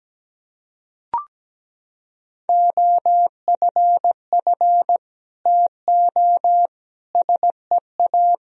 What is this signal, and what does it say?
Morse code, heard as a keyed tone, about 17 wpm, 700 Hz. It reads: OFF TO SEA
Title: Sentences from Top 1000 Words - Review Part 1 - 17wpm